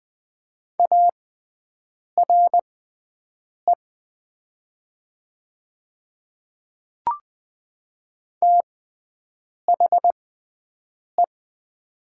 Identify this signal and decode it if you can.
Morse code, heard as a keyed tone, about 20 wpm, spaced as for 7 wpm, 700 Hz. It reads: ARE THE
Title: 6x Character Spacing - Top 500 Words - 20wpm